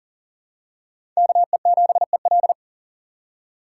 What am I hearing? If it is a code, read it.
Morse code, heard as a keyed tone, about 40 wpm, 700 Hz. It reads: KE7EL
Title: USA Call Signs - 5 Characters - 3x Character Spacing - 40wpm